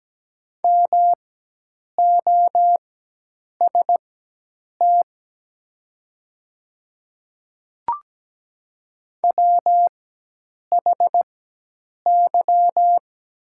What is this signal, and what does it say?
Morse code, heard as a keyed tone, about 17 wpm, spaced as for 8 wpm, 700 Hz. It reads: MOST WHY
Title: 4x Character Spacing - Top 200 Words - 17wpm